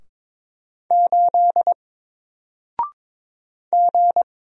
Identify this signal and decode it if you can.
Morse code, heard as a keyed tone, about 22 wpm, 700 Hz. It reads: 8 G
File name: Single-Letter-Number-22wpm